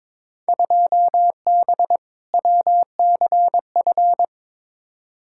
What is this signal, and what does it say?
Morse code, heard as a keyed tone, about 22 wpm, 700 Hz. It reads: 2B WCF
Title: Field Day Exchanges Encore - 22wpm